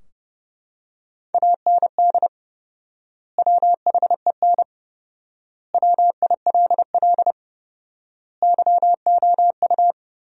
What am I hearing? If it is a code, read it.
Morse code, heard as a keyed tone, about 30 wpm, 700 Hz. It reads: AND WHEN WILL YOU
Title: 4x Word-Spacing - Sentences from Top 100 Words - 30wpm